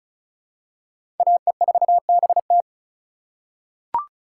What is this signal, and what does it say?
Morse code, heard as a keyed tone, about 34 wpm, 700 Hz. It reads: AE4BT
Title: USA Call Signs - 5 Characters - 3x Character Spacing - 35wpm